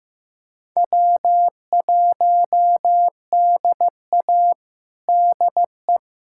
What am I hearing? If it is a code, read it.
Morse code, heard as a keyed tone, about 15 wpm, 700 Hz. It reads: W1DA DE
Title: USA Callsign DE Callsign - 4 Characters - 15wpm